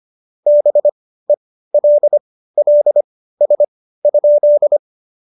Morse code, heard as a keyed tone, about 25 wpm, spaced as for 15 wpm, 590 Hz. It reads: BELLS?
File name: Jokes-25-char-15-eff